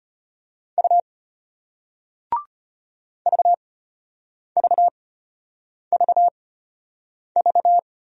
Morse code, heard as a keyed tone, about 34 wpm, spaced as for 8 wpm, 700 Hz. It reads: U VVVV